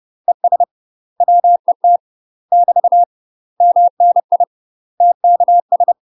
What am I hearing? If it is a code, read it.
Morse code, heard as a keyed tone, about 30 wpm, 700 Hz. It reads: ES WET <BT> MNI TKS
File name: UK MASTER QSOs TRAINING / 30WPM / UK MASTER QSOs 0046